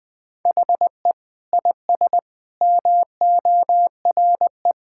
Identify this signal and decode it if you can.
Morse code, heard as a keyed tone, about 20 wpm, 700 Hz. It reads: HE IS MORE
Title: Sentences with 4-Word Phrases - Spoken Beginning - 20wpm